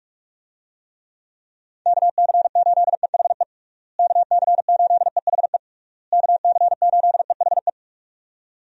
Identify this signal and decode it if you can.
Morse code, heard as a keyed tone, about 46 wpm, 700 Hz. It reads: KC8EHE KC8EHE KC8EHE
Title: USA Call Signs - 6 Characters - Repeated 3x - 45wpm